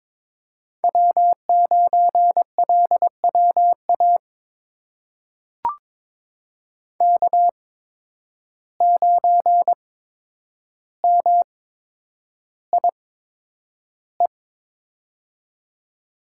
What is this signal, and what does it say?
Morse code, heard as a keyed tone, about 22 wpm, spaced as for 9 wpm, 700 Hz. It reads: W9LWA K9MIE
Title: USA Call Signs - 5 Characters - 8x Character Spacing - 22wpm